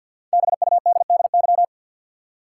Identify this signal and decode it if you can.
Morse code, heard as a keyed tone, about 47 wpm, 700 Hz. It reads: BUDDY
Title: Five Letter Words - 50wpm